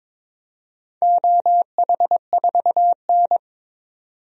Morse code, heard as a keyed tone, about 22 wpm, 700 Hz. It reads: OH4N
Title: DXCC-100 Call Signs - 4 Characters - 22wpm